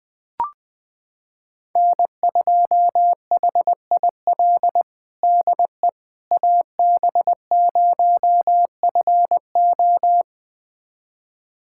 Morse code, heard as a keyed tone, about 21 wpm, 700 Hz. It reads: N2HIL DE AB0FO
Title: USA Callsign DE Callsign - 5 Characters - 20wpm